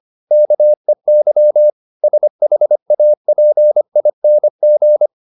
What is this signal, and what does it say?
Morse code, heard as a keyed tone, about 25 wpm, 600 Hz. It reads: KEY SHAPING